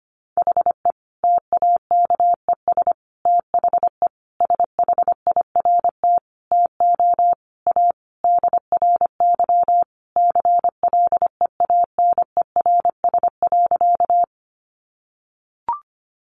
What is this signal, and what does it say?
Morse code, heard as a keyed tone, about 25 wpm, 700 Hz. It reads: HE TAKES THE SHIRT TO A DRY CLEANERS.
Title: Additional Easy Phrases 25wpm